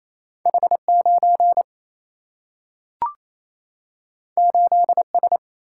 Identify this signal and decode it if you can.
Morse code, heard as a keyed tone, about 28 wpm, 700 Hz. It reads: H9 8S
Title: Single-Letter Single-Number 28wpm